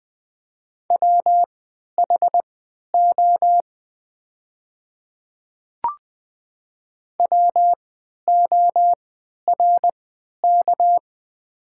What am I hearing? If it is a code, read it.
Morse code, heard as a keyed tone, about 20 wpm, spaced as for 11 wpm, 700 Hz. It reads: WHO WORK
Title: 3x Character Spacing - Top 500 Words - 20wpm